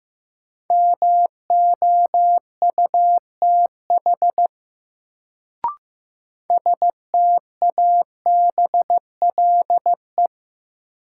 Morse code, heard as a keyed tone, about 15 wpm, 700 Hz. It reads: MOUTH STABLE
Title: Sentences from Top 1500 Words - New Word Drill - 15wpm